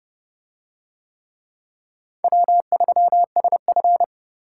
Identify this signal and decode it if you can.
Morse code, heard as a keyed tone, about 28 wpm, 700 Hz. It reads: W3SF